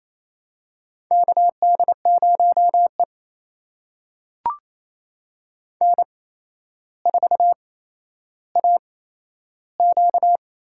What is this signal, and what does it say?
Morse code, heard as a keyed tone, about 28 wpm, spaced as for 12 wpm, 700 Hz. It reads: KD0E N4AQ
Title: USA Call Signs - 4 Characters - 8x Character Spacing - 28wpm